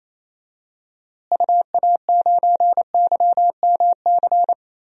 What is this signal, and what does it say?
Morse code, heard as a keyed tone, about 28 wpm, 700 Hz. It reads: UA9YMC
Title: DXCC-100 Call Signs - 6 Characters - Repeated 2x - 28wpm